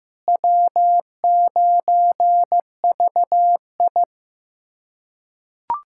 Morse code, heard as a keyed tone, about 15 wpm, 700 Hz. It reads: W9VI
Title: USA Callsign DE Callsign - 4 Characters - 15wpm